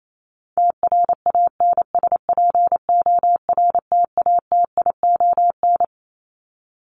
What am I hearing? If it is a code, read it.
Morse code, heard as a keyed tone, about 28 wpm, 700 Hz. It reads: TRANSPORTATION
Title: Top 1501-1750 Words 28wpm